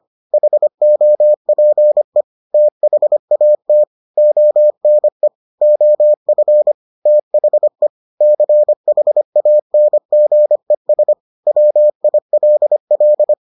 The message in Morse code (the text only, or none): HOPE THAT ONE OF THE CHANGES WILL